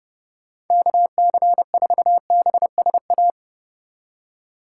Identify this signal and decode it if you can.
Morse code, heard as a keyed tone, about 32 wpm, 700 Hz. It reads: KC4BSA